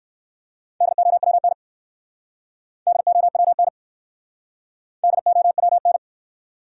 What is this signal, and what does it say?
Morse code, heard as a keyed tone, about 63 wpm, 700 Hz. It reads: DOWN DOWN DOWN